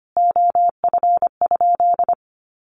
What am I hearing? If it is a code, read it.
Morse code, heard as a keyed tone, about 25 wpm, 700 Hz. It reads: OF?
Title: Sentences from Top 1400 Words 25wpm